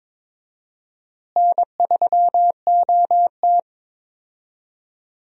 Morse code, heard as a keyed tone, about 21 wpm, 700 Hz. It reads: N3OT
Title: USA Call Signs - 4 Characters - Repeated 3x - 22wpm